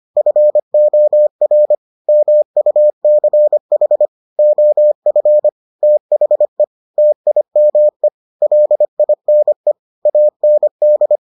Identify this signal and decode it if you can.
Morse code, heard as a keyed tone, about 25 wpm, 600 Hz. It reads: FOR MUCH OF THE TIME LINE AND